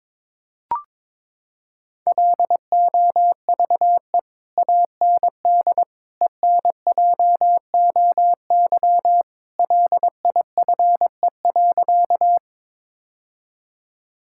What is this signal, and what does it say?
Morse code, heard as a keyed tone, about 22 wpm, 700 Hz. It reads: LOVE AND ENJOY LIFE.
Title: Sentences from Top 1000 Words - Review Part 5 - 22wpm